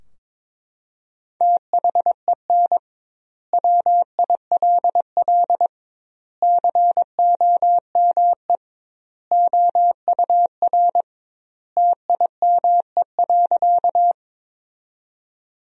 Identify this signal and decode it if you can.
Morse code, heard as a keyed tone, about 22 wpm, 700 Hz. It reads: THEN WILL COME OUR TIME.
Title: 2x Word-Spacing - Sentences from Top 200 Words - 22wpm